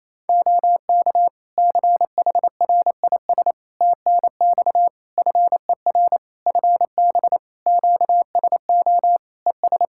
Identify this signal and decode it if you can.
Morse code, heard as a keyed tone, about 28 wpm, 700 Hz. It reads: OK CHRIS TNX FER FB QSO ES
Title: Practice QSO - Exchange 4 - Ending - 28wpm